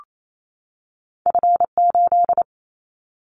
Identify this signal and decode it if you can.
Morse code, heard as a keyed tone, about 28 wpm, 700 Hz. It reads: F8
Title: Callsign Prefixes for Top 25 DXCC - 28wpm